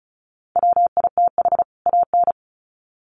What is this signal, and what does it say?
Morse code, heard as a keyed tone, about 34 wpm, 700 Hz. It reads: WITH AN
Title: Sets of 2 Words 35wpm